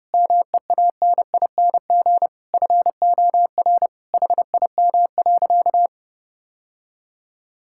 Morse code, heard as a keyed tone, about 30 wpm, 700 Hz. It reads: MEANING FOR HIM.
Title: Sentences from Top 1000 Words - Review Part 6 - 30wpm